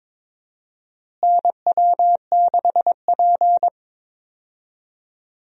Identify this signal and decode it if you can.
Morse code, heard as a keyed tone, about 22 wpm, 700 Hz. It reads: NW6P
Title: USA Call Signs - 4 Characters - Repeated 3x - 22wpm